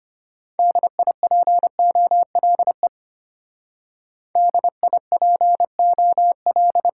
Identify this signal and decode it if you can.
Morse code, heard as a keyed tone, about 33 wpm, 700 Hz. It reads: DIPOLE DIPOL